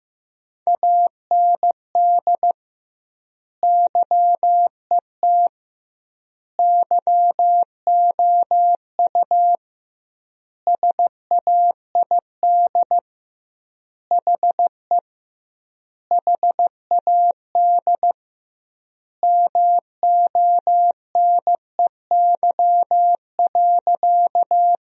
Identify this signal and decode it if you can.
Morse code, heard as a keyed tone, about 16 wpm, 700 Hz. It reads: AND YET YOU SAID HE HAD MONEY.
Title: 2x Word-Spacing - Sentences from Top 500 Words - 15wpm